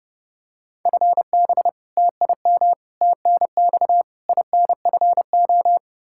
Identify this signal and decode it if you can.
Morse code, heard as a keyed tone, about 32 wpm, 700 Hz. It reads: FB TIM TNX INFO
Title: K1USN SST QSO Practice 30wpm